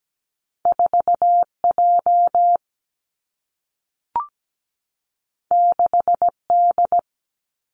Morse code, heard as a keyed tone, about 17 wpm, 700 Hz. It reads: 4 J 6 D
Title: Two Challenging Letter-Number Combinations 17wpm